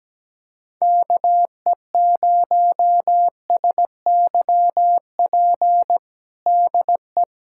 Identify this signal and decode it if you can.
Morse code, heard as a keyed tone, about 17 wpm, 700 Hz. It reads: KE0SYP DE
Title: USA Callsign DE Callsign 17wpm